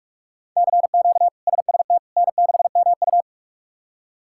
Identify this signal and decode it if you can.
Morse code, heard as a keyed tone, about 41 wpm, 700 Hz. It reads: CQ SST N6MU
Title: K1USN SST QSO Practice - 45wpm